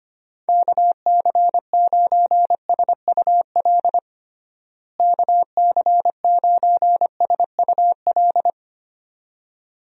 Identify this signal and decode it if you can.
Morse code, heard as a keyed tone, about 25 wpm, 700 Hz. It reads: KC9SUL KC9SUL
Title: USA Call Signs - 6 Characters - Repeated 2x - 25wpm